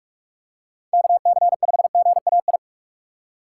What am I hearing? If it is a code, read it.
Morse code, heard as a keyed tone, about 46 wpm, 700 Hz. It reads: KC5GAI